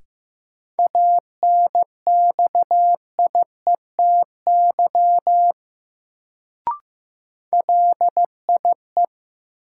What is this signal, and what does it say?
Morse code, heard as a keyed tone, about 15 wpm, 700 Hz. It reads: ANXIETY LIE
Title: Sentences from Top 1600 Words - New Word Drill - 15wpm